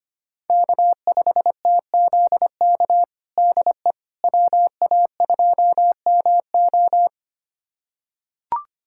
Morse code, heard as a keyed tone, about 25 wpm, 700 Hz. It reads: K5TZK DE WA2MO
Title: USA Callsign DE Callsign - 5 Characters - 25wpm